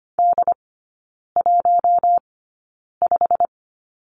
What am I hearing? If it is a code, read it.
Morse code, heard as a keyed tone, about 25 wpm, spaced as for 9 wpm, 690 Hz. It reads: D15
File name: Lesson 15e - Number 5 - Three Characters 25wpm